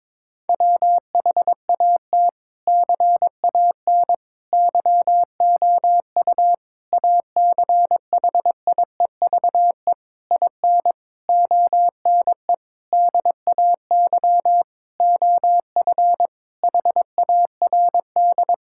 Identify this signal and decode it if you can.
Morse code, heard as a keyed tone, about 22 wpm, 700 Hz. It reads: WHAT CAN YOU ACHIEVE IN ONE DAY OF HARD
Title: Sentences from Top 1000 Words - Review Part 6 - 22wpm